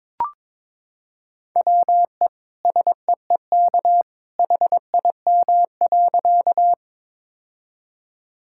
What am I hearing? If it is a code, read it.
Morse code, heard as a keyed tone, about 23 wpm, 700 Hz. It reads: WE SEEK HIM.
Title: Sentences from Top 1600 Words 22wpm